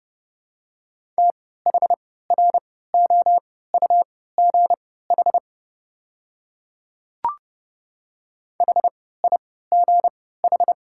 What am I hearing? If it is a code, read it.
Morse code, heard as a keyed tone, about 32 wpm, spaced as for 17 wpm, 700 Hz. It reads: THROUGH HIGH
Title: 3x Character Spacing - Top 200 Words - 30wpm